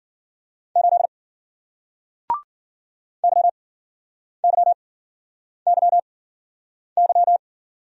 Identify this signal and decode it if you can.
Morse code, heard as a keyed tone, about 54 wpm, spaced as for 9 wpm, 700 Hz. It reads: 8 YYYY